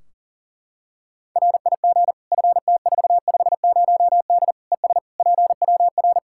Morse code, heard as a keyed tone, about 40 wpm, 700 Hz. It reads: RIG FT450D ES PWR
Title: Practice QSO - Exchange 2 - Info - 40wpm